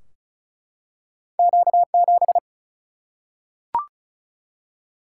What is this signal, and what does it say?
Morse code, heard as a keyed tone, about 36 wpm, 700 Hz. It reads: Q 7
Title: Single-Letter Single-Number 35wpm